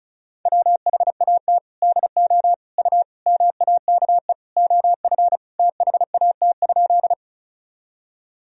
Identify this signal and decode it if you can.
Morse code, heard as a keyed tone, about 36 wpm, 700 Hz. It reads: WHAT DO U MAKE OF THAT?
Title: Sentences from Top 100 Words and Abbreviations 35wpm